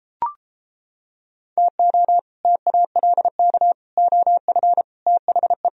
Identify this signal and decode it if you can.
Morse code, heard as a keyed tone, about 29 wpm, 700 Hz. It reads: TO TALK OF THE